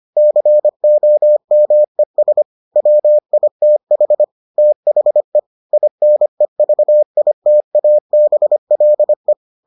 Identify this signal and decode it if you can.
Morse code, heard as a keyed tone, about 25 wpm, 600 Hz. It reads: COMES WITH THE INEVITABLE